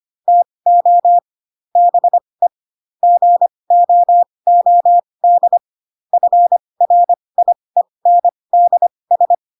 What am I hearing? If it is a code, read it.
Morse code, heard as a keyed tone, about 25 wpm, spaced as for 20 wpm, 700 Hz. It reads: TO BE GOOD FRIENDS
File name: Pinocchio / Pinocchio ch2 15 25 ik1hge